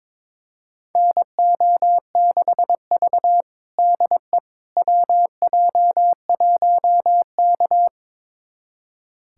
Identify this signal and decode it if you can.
Morse code, heard as a keyed tone, about 22 wpm, 700 Hz. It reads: NO6V DE WJ1K
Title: USA Callsign DE Callsign - 4 Characters - 22wpm